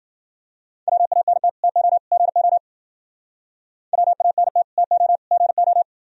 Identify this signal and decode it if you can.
Morse code, heard as a keyed tone, about 52 wpm, 700 Hz. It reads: WANT TO GO WANT TO GO